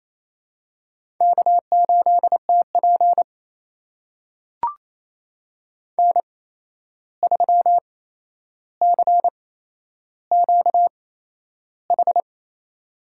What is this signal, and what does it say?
Morse code, heard as a keyed tone, about 28 wpm, spaced as for 11 wpm, 700 Hz. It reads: K8TP N3CQH
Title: USA Call Signs - 8x Character Spacing - 28wpm